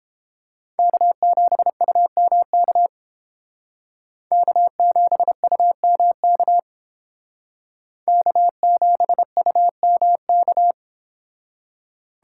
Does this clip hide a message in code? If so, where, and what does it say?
Yes: Morse code from 0.8 to 10.7 s, heard as a keyed tone, about 31 wpm, 700 Hz. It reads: K7UMK K7UMK K7UMK